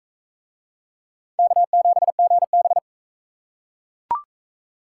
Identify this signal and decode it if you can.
Morse code, heard as a keyed tone, about 40 wpm, 700 Hz. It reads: K7GB